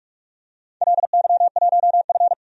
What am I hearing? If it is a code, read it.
Morse code, heard as a keyed tone, about 46 wpm, 700 Hz. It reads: LY1F
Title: DXCC-100 Call Signs - 4 Characters - 45wpm